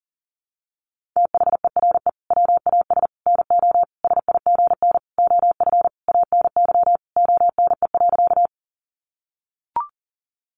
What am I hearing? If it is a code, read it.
Morse code, heard as a keyed tone, about 37 wpm, 700 Hz. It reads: THERE WAS NO SIGN OF ANY ONE.
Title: Sentences from Top 1000 Words 40wpm